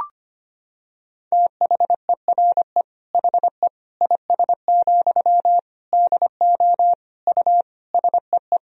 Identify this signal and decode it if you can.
Morse code, heard as a keyed tone, about 25 wpm, 700 Hz. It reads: THERE HE IS, DO U SEE
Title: Sentences from Top 100 Words and Abbreviations 25wpm